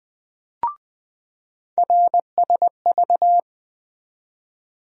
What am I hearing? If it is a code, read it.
Morse code, heard as a keyed tone, about 20 wpm, 700 Hz. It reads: RSV
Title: USA Call Sign Postfixes - 3 Letters 20wpm